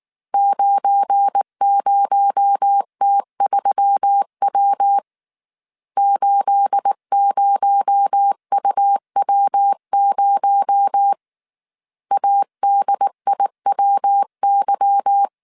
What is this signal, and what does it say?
Morse code, heard as a keyed tone, about 19 wpm, 800 Hz. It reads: 90T3W 80UW0 ADIWY
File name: CW / DET / 1 9 1500C S A 18wpm 2459